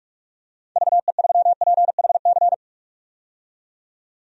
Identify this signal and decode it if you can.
Morse code, heard as a keyed tone, about 46 wpm, 700 Hz. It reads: VE3PHC
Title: DXCC-100 Call Signs - 6 Characters - 45wpm